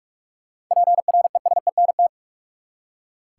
Morse code, heard as a keyed tone, about 42 wpm, 700 Hz. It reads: PRESENT
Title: Seven Letter Words - 45wpm